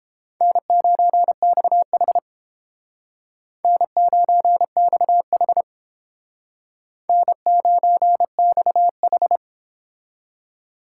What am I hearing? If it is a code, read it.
Morse code, heard as a keyed tone, about 31 wpm, 700 Hz. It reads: N9XH N9XH N9XH